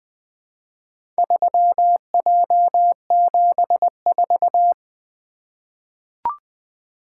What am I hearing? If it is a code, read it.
Morse code, heard as a keyed tone, about 20 wpm, 700 Hz. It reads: 3J74
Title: Four Letter-Number Combinations - 20wpm